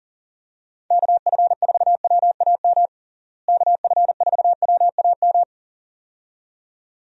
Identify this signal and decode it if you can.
Morse code, heard as a keyed tone, about 40 wpm, 700 Hz. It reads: KF4WAM KF4WAM
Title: USA Call Signs - 6 Characters - Repeated 2x - 40wpm